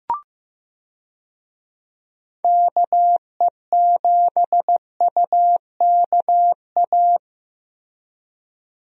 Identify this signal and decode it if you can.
Morse code, heard as a keyed tone, about 15 wpm, 700 Hz. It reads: KE7UKA
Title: USA Call Signs - 6 Characters - Repeated 2x - 15wpm